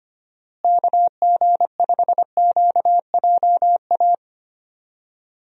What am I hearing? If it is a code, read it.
Morse code, heard as a keyed tone, about 25 wpm, 700 Hz. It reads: KG5QJA
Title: USA Call Signs - 6 Characters - Repeated 3x - 25wpm